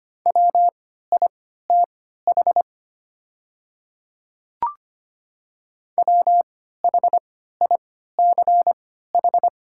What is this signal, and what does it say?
Morse code, heard as a keyed tone, about 25 wpm, spaced as for 14 wpm, 700 Hz. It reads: WITH WHICH
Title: 3x Character Spacing - Top 100 Words - 25wpm